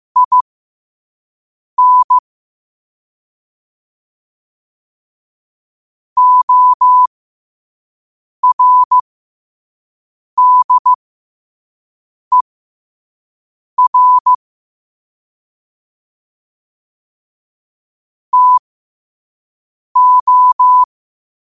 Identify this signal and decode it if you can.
Morse code, heard as a keyed tone, about 15 wpm, spaced as for 5 wpm, 1000 Hz. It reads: IN ORDER TO
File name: 5wpm